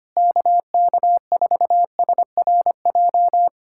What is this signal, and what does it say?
Morse code, heard as a keyed tone, about 24 wpm, 700 Hz. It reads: KK4SRJ
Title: Speed-Racing Common QSO Phrases Encore - 25wpm